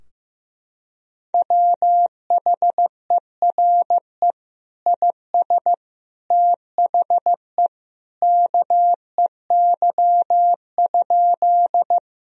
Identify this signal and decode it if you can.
Morse code, heard as a keyed tone, about 15 wpm, 700 Hz. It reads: WHERE IS THE KEY?
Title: Sentences from Top 1000 Words - Review Part 4 - 15wpm